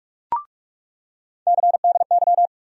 Morse code, heard as a keyed tone, about 46 wpm, 700 Hz. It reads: CDY
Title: USA Call Sign Postfixes - 3 Letters 45wpm